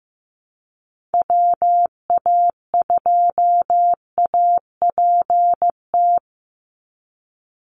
Morse code, heard as a keyed tone, about 15 wpm, 700 Hz. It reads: WA2APT
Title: USA Call Signs 15wpm